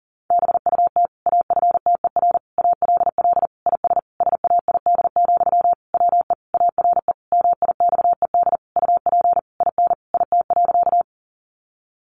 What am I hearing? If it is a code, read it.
Morse code, heard as a keyed tone, about 40 wpm, 700 Hz. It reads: BUT AFTER ALL IS SAID, WE ARE MIXED UP IN IT.
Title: Sentences from Top 1500 Words 40wpm